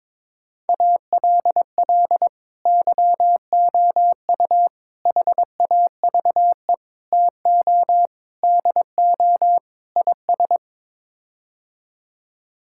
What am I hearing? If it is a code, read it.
Morse code, heard as a keyed tone, about 22 wpm, 700 Hz. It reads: ALL YOU HAVE TO DO IS
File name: practice / 22wpm / Sets of 6 Words - 22wpm